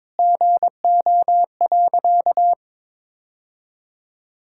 Morse code, heard as a keyed tone, about 21 wpm, 700 Hz. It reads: GO.
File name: practice / 22wpm / Sentences with 2-Word Phrases - 22wpm